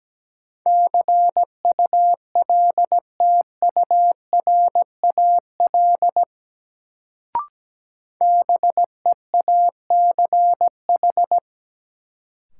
Morse code, heard as a keyed tone, about 17 wpm, 700 Hz. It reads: CULTURAL BEACH
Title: Sentences from Top 1500 Words - New Word Drill Alt - 17wpm